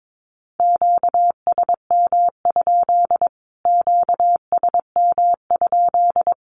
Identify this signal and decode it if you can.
Morse code, heard as a keyed tone, about 22 wpm, 700 Hz. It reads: QSM? QSM?
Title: Q-Codes Repeated 3x 22wpm